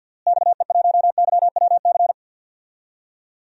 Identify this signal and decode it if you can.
Morse code, heard as a keyed tone, about 52 wpm, 700 Hz. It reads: XE1YWC
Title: DXCC-100 Call Signs - 6 Characters - Repeated 3x - 50wpm